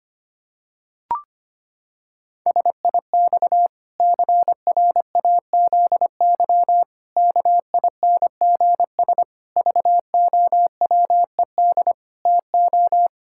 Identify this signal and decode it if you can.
Morse code, heard as a keyed tone, about 25 wpm, 700 Hz. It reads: SIX CRAZY KINGS VOWED TO
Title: Pangram Challenge v2 - 25wpm